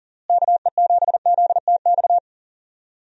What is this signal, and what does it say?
Morse code, heard as a keyed tone, about 38 wpm, 700 Hz. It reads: KE7ZTX
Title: USA Call Signs - 6 Characters - Repeated 3x - 40wpm